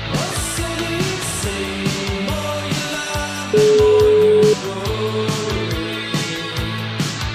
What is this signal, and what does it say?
Ringback tone (Europe, 425 Hz), with noise.